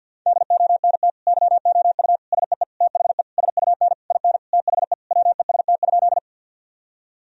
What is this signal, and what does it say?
Morse code, heard as a keyed tone, about 50 wpm, 700 Hz. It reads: DONT YOU SEE THE SUN IN THE WEST?
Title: Sentences from Top 1000 Words - Review Part 2 - 50wpm